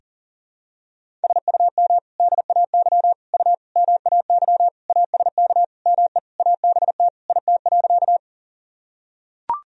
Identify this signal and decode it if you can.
Morse code, heard as a keyed tone, about 40 wpm, 700 Hz. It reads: SUM DAY U MAY ASK ME ABT IT.
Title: Sentences from Top 200 Words and Abbreviations 40wpm